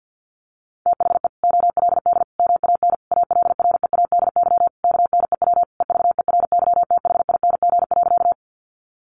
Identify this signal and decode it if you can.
Morse code, heard as a keyed tone, about 48 wpm, 700 Hz. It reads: THE OLD MAN ALREADY KNEW EVERYTHING.
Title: Sentences from Top 700 Words 50wpm